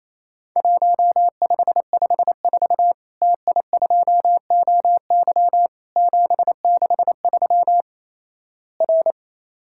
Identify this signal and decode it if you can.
Morse code, heard as a keyed tone, about 28 wpm, 700 Hz. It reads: 1554 TI2OY 763 R
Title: Worked All Europe DX Contest - QTC Exchange - 28wpm